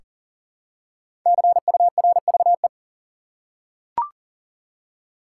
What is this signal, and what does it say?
Morse code, heard as a keyed tone, about 40 wpm, 700 Hz. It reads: CURVE
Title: Five Letter Words - 40wpm